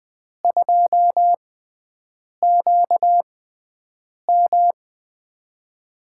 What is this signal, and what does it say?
Morse code, heard as a keyed tone, about 20 wpm, spaced as for 7 wpm, 700 Hz. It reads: 2QM
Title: USA Call Signs - 5 Characters - 6x Character Spacing - 20wpm